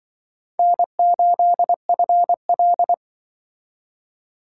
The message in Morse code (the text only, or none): N8FL